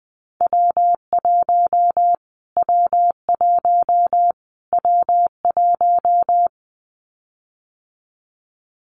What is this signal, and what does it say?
Morse code, heard as a keyed tone, about 20 wpm, 700 Hz. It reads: W1 W1 W1
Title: Callsign Prefixes for Top 25 DXCC - Repeated 3x - 20wpm